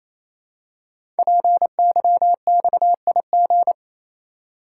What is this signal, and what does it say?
Morse code, heard as a keyed tone, about 29 wpm, 700 Hz. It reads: PYXIG
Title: Code Groups - 5 Letters - 28wpm